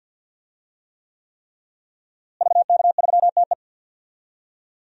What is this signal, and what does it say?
Morse code, heard as a keyed tone, about 51 wpm, 700 Hz. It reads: VK3TE
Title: DXCC-100 Call Signs - 5 Characters - Repeated 3x - 50wpm